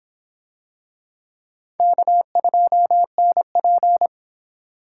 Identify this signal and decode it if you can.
Morse code, heard as a keyed tone, about 24 wpm, 700 Hz. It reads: K2NP